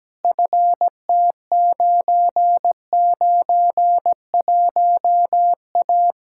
Morse code, heard as a keyed tone, about 17 wpm, 700 Hz. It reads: FT991A
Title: Practice QSO - Exchange 2 - Info - 17wpm